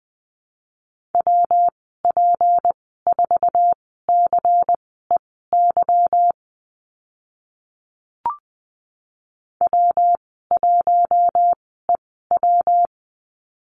Morse code, heard as a keyed tone, about 20 wpm, spaced as for 15 wpm, 700 Hz. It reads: WP4CEY W1EW
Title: 161 Lesson 26g - Letter g - Call Signs 20wpm